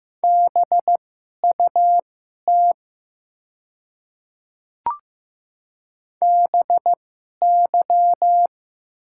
Morse code, heard as a keyed tone, about 15 wpm, spaced as for 11 wpm, 700 Hz. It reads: BUT BY
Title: 2x Character Spacing - Top 100 Words - 15wpm